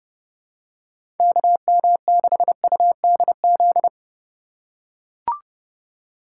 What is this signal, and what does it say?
Morse code, heard as a keyed tone, about 30 wpm, 700 Hz. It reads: KM6UDZ
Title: USA Call Signs - 3x Character Spacing - 30wpm